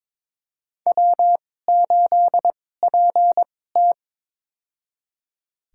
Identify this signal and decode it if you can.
Morse code, heard as a keyed tone, about 23 wpm, spaced as for 16 wpm, 700 Hz. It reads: W8PT
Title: USA Call Signs - 4 Characters - 2x Character Spacing - 22wpm